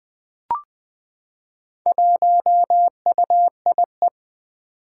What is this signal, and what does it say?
Morse code, heard as a keyed tone, about 20 wpm, 700 Hz. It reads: 1UIE
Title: Four Challenging Letter-Number Combinations 20wpm